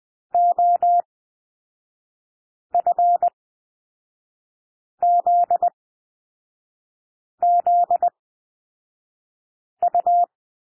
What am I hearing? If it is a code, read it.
Morse code, heard as a keyed tone, about 20 wpm, spaced as for 5 wpm, 700 Hz. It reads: OFZZU
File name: SOUND LETTERS 20 wpm